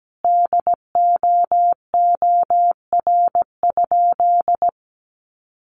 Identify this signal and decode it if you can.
Morse code, heard as a keyed tone, about 17 wpm, 700 Hz. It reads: DOOR?
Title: Sentences from Top 900 Words 17wpm